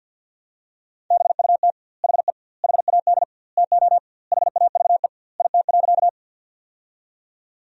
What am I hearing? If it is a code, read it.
Morse code, heard as a keyed tone, about 50 wpm, 700 Hz. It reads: BUT HE HAD TO HAVE IT.
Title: 2x Word-Spacing - Sentences from Top 100 Words - 50wpm